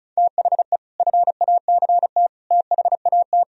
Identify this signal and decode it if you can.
Morse code, heard as a keyed tone, about 45 wpm, 700 Hz. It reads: THE FACT THAT